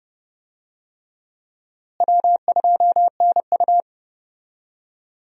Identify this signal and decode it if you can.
Morse code, heard as a keyed tone, about 29 wpm, 700 Hz. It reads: W2NU
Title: USA Call Signs - 4 Characters - Repeated 3x - 30wpm